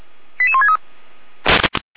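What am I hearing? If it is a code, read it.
Five-tone selective call (ZVEI), heard as quick runs of five tones, tones of about 70 ms. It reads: 88163
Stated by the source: ZVEI five-tone selective call received by SDR.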